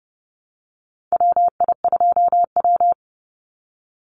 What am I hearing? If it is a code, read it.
Morse code, heard as a keyed tone, about 31 wpm, 700 Hz. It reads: WI2W
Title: NA SOTA Chaser Callsigns 30wpm